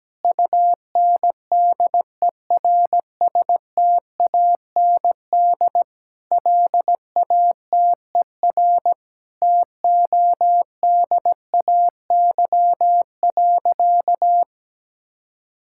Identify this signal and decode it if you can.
Morse code, heard as a keyed tone, about 17 wpm, 700 Hz. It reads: UNDERSTAND LATER TODAY.
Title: Sentences from Top 1000 Words - Review Part 5 - 17wpm